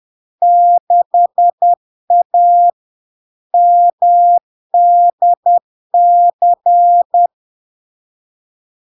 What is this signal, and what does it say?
Morse code, heard as a keyed tone, about 10 wpm, 700 Hz. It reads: 6A MDC
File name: LARK-FD-2022-Log-in-CW-10wpm-700-Hz